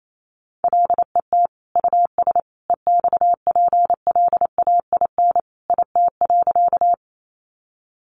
Morse code, heard as a keyed tone, about 28 wpm, 700 Hz. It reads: LET US EXPLAIN IT.
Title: Sentences from Top 1100 Words 28wpm